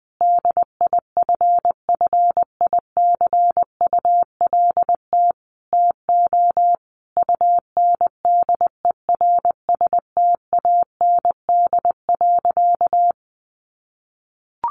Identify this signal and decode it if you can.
Morse code, heard as a keyed tone, about 20 wpm, 700 Hz. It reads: DIFFICULT TO UNDERSTAND.
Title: Sentences from Top 1100 Words 20wpm